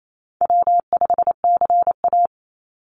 Morse code, heard as a keyed tone, about 28 wpm, 700 Hz. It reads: W5CA
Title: USA Call Signs 28wpm